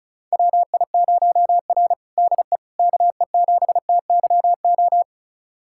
Morse code, heard as a keyed tone, about 34 wpm, 700 Hz. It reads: WI0R DE KE7TYO
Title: USA Callsign DE Callsign 35wpm